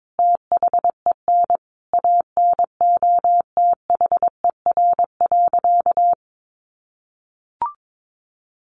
Morse code, heard as a keyed tone, about 22 wpm, 700 Hz. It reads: THEN ANOTHER.
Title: Sentences from Top 1400 Words 22wpm